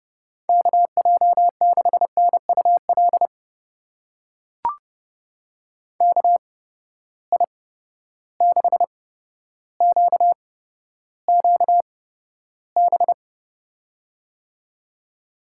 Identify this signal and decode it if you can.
Morse code, heard as a keyed tone, about 30 wpm, spaced as for 13 wpm, 700 Hz. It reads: KJ6NUL KI6QQB
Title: USA Call Signs - 6 Characters - 8x Character Spacing - 30wpm